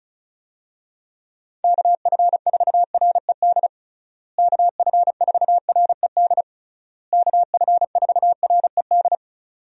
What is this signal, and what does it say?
Morse code, heard as a keyed tone, about 36 wpm, 700 Hz. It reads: KF4RED KF4RED KF4RED
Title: USA Call Signs - 6 Characters - Repeated 3x - 35wpm